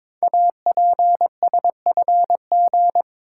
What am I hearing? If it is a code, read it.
Morse code, heard as a keyed tone, about 22 wpm, 700 Hz. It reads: APSFG
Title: Code Groups - 5 Letters - 22wpm